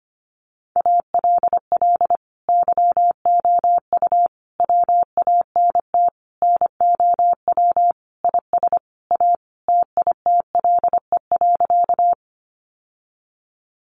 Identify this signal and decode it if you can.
Morse code, heard as a keyed tone, about 25 wpm, 700 Hz. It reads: ALL YOU WANT NOW IS A TITLE.
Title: Sentences from Top 1200 Words 25wpm